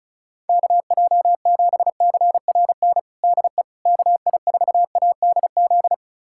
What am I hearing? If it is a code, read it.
Morse code, heard as a keyed tone, about 34 wpm, 700 Hz. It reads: KJ7CRN DE KI4ADZ
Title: USA Callsign DE Callsign - 6 Characters - 35wpm